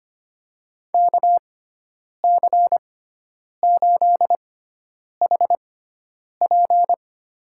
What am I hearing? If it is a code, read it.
Morse code, heard as a keyed tone, about 25 wpm, spaced as for 9 wpm, 700 Hz. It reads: KC8HP
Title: USA Call Signs - 5 Characters - 6x Character Spacing - 25wpm